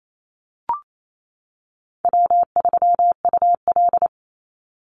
Morse code, heard as a keyed tone, about 28 wpm, 700 Hz. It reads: W3UL
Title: Easy Call Signs Repeated 2x - 28wpm